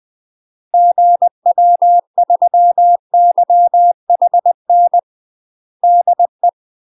Morse code, heard as a keyed tone, about 20 wpm, 700 Hz. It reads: GW3YHN DE
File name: UK MASTER QSOs TRAINING / 20WPM / UK MASTER QSOs 0074